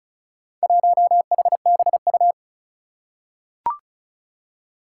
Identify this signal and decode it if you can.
Morse code, heard as a keyed tone, about 35 wpm, 700 Hz. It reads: 1HBU
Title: Four Challenging Letter-Number Combinations 35wpm